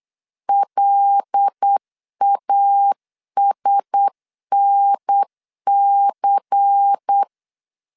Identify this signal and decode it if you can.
Morse code, heard as a keyed tone, about 8 wpm, 800 Hz. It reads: LASNC
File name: LEZIONI / 6 PKX0 / 6 PKX0 8wpm 3341